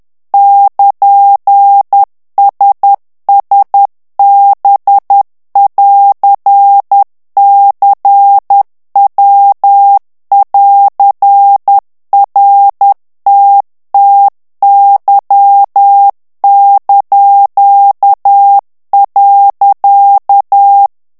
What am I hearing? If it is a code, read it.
Morse code, heard as a keyed tone, about 10 wpm, 800 Hz. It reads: <KN>SSB<AR>CW<AR>RTTY).
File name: ITALIEN texte clair 2 vitesse 10 mots minute